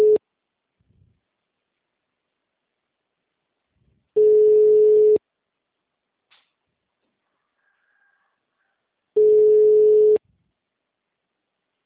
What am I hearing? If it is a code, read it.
Ringback tone (Europe, 425 Hz).